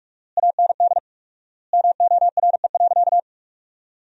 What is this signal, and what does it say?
Morse code, heard as a keyed tone, about 46 wpm, 700 Hz. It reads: AND MORE.
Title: 4x Word-Spacing - Sentences from Top 500 Words - 45wpm